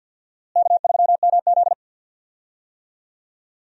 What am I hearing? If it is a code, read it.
Morse code, heard as a keyed tone, about 51 wpm, 700 Hz. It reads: K3MZ
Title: USA Call Signs - 4 Characters - Repeated 2x - 50wpm